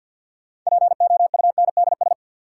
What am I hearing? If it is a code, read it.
Morse code, heard as a keyed tone, about 53 wpm, 700 Hz. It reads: POUNDS
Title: Sentences from Top 1000 Words - Review Part 2 - New Word Drill - 50wpm